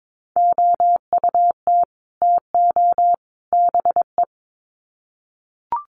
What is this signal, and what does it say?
Morse code, heard as a keyed tone, about 22 wpm, 700 Hz. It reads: OUT TO BE
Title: Sets of 3 Words 22wpm